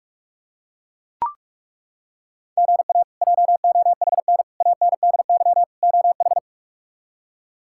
Morse code, heard as a keyed tone, about 42 wpm, 700 Hz. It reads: GA JOHN ANDY OH
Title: K1USN SST QSO Practice - 45wpm